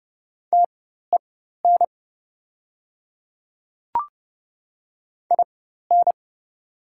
Morse code, heard as a keyed tone, about 31 wpm, spaced as for 14 wpm, 690 Hz. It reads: TEN IN
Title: 4x Character Spacing - Top 500 Words - 30wpm